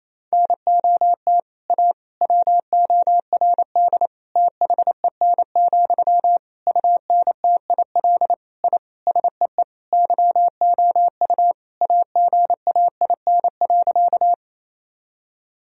Morse code, heard as a keyed tone, about 28 wpm, 700 Hz. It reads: NOT A WORD THEN, UNTIL I SEE YOU AGAIN.
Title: Sentences from Top 1000 Words - Review Part 1 - 28wpm